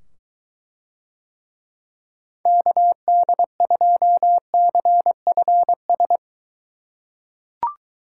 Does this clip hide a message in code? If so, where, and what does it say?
Yes: Morse code from 2.5 to 6.2 s, heard as a keyed tone, about 23 wpm, 700 Hz. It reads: KD2CFS